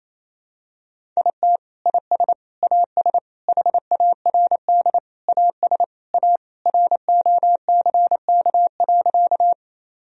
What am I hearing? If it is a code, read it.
Morse code, heard as a keyed tone, about 28 wpm, 700 Hz. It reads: IT IS AS HARD AS A ROCK.
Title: Sentences from Top 1000 Words - Review Part 2 - 28wpm